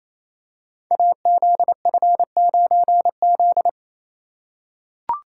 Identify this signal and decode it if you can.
Morse code, heard as a keyed tone, about 29 wpm, 700 Hz. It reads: AZF9Z
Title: Code Groups - 5 Letter:Numbers - 28wpm